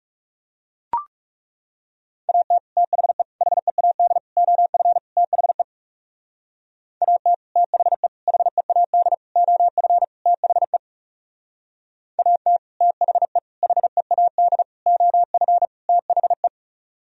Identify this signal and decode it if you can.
Morse code, heard as a keyed tone, about 38 wpm, 700 Hz. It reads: AT THE HEAD OF THE AT THE HEAD OF THE AT THE HEAD OF THE